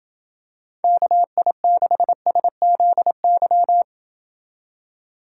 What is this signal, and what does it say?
Morse code, heard as a keyed tone, about 27 wpm, 700 Hz. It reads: KI6SZY